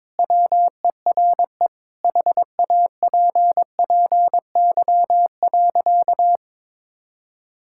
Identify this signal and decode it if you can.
Morse code, heard as a keyed tone, about 21 wpm, 700 Hz. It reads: WERE HAPPY.
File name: practice / 22wpm / Sentences with 2-Word Phrases - 22wpm